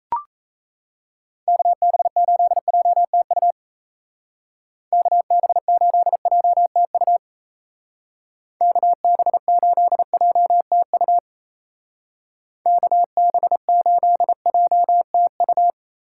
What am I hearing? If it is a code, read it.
Morse code, heard as a keyed tone, about 35 wpm, 700 Hz. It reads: KB8JTU KB8JTU KB8JTU KB8JTU
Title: Speed Racing - USA 2x3 Call Signs 28wpm